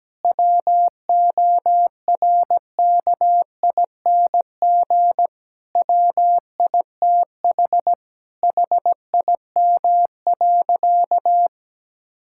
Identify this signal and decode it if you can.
Morse code, heard as a keyed tone, about 17 wpm, 700 Hz. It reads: WORKING WITH HIM.
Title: Sentences from Top 1000 Words - Review Part 5 - 17wpm